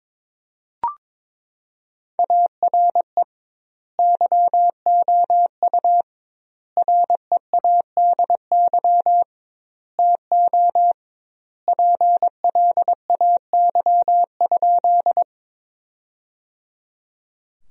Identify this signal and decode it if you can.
Morse code, heard as a keyed tone, about 22 wpm, 700 Hz. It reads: ARE YOU READY TO PLAY?
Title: 2x Word-Spacing - Sentences from Top 400 Words - 22wpm